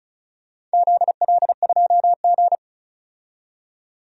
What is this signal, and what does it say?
Morse code, heard as a keyed tone, about 36 wpm, 700 Hz. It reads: ZL2G
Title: DXCC-100 Call Signs - 4 Characters - 35wpm